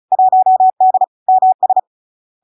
Morse code, heard as a keyed tone, about 36 wpm, 750 Hz. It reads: 1D MS
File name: LARK-FD-2022-Log-in-CW-35wpm-750-Hz